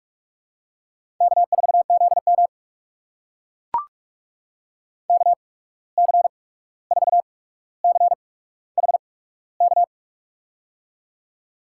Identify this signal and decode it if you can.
Morse code, heard as a keyed tone, about 44 wpm, spaced as for 17 wpm, 700 Hz. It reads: K4ZM KC4CHK
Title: USA Call Signs - 8x Character Spacing - 45wpm